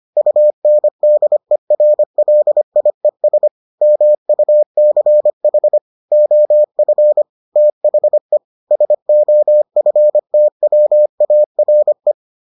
Morse code, heard as a keyed tone, about 25 wpm, 600 Hz. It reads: UNDERLIES MUCH OF THE SOFTWARE